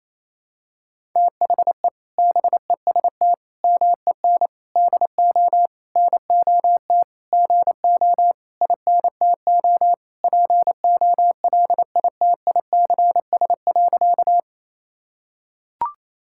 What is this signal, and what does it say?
Morse code, heard as a keyed tone, about 28 wpm, 700 Hz. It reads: THE BEST MEN DO NOT GO INTO POLITICS.
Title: Sentences from Top 1000 Words - Review Part 6 - 28wpm